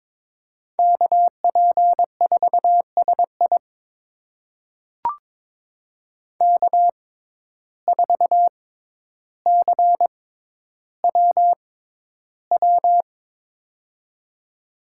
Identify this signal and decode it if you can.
Morse code, heard as a keyed tone, about 22 wpm, spaced as for 11 wpm, 700 Hz. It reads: KP4SI K4CWW
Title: USA Call Signs - 5 Characters - 6x Character Spacing - 22wpm